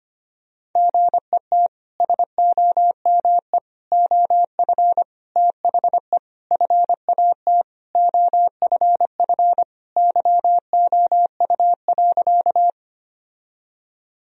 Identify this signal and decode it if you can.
Morse code, heard as a keyed tone, about 25 wpm, 700 Hz. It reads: GET SOME OF THE FAT OFF YOU.
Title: Sentences from Top 1000 Words - Review Part 4 - 25wpm